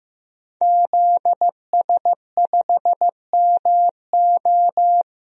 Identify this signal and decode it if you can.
Morse code, heard as a keyed tone, about 15 wpm, 700 Hz. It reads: ZS5MO
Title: DXCC-100 Call Signs - 5 Characters - 15wpm